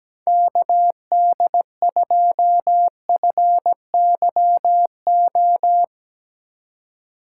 Morse code, heard as a keyed tone, about 17 wpm, 700 Hz. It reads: KD2FYO
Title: USA Call Signs - 6 Characters - 4x Character Spacing - 17wpm